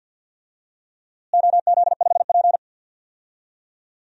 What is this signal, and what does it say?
Morse code, heard as a keyed tone, about 53 wpm, 700 Hz. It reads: OZ5P
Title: DXCC-100 Call Signs - 4 Characters - Repeated 3x - 50wpm